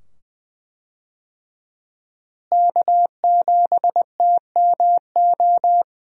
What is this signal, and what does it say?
Morse code, heard as a keyed tone, about 20 wpm, 700 Hz. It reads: K7TMO
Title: USA Call Signs - 5 Characters - Repeated 2x - 20wpm